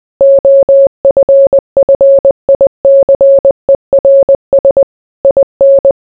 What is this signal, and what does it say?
Morse code, heard as a keyed tone, about 20 wpm, 550 Hz. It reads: OFFICERS IN